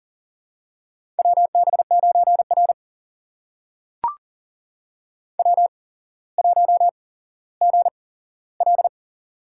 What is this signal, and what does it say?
Morse code, heard as a keyed tone, about 43 wpm, spaced as for 17 wpm, 700 Hz. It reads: WB9R W1GL
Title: USA Call Signs - 4 Characters - 8x Character Spacing - 40wpm